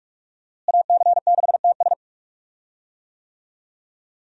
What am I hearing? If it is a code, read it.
Morse code, heard as a keyed tone, about 41 wpm, 700 Hz. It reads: AC6TS
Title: USA Call Signs - 5 Characters - Repeated 2x - 45wpm